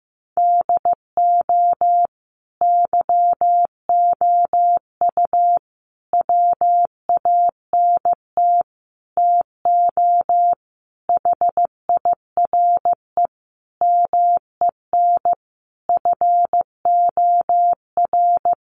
Morse code, heard as a keyed tone, about 15 wpm, 700 Hz. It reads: DO YOU WANT TO HIRE MEN FOR
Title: Sentences from Top 1400 Words 15wpm